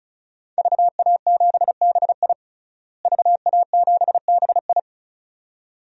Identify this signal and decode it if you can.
Morse code, heard as a keyed tone, about 36 wpm, 700 Hz. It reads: VA7BI VA7BI
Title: DXCC-100 Call Signs - 5 Characters - Repeated 2x - 35wpm